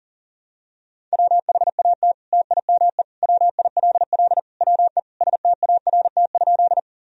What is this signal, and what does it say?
Morse code, heard as a keyed tone, about 40 wpm, 700 Hz. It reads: WHAT TIME WILL WE START?
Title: Sentences from Top 1000 Words - Review Part 1 - 40wpm